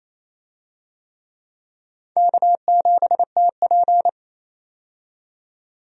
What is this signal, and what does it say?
Morse code, heard as a keyed tone, about 28 wpm, 700 Hz. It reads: K7TP
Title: USA Call Signs - 4 Characters - 28wpm